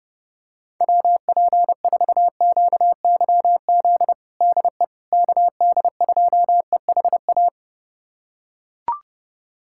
Morse code, heard as a keyed tone, about 32 wpm, 700 Hz. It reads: WP4QYZ DE KD2EHA
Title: USA Callsign DE Callsign - 6 Characters - 30wpm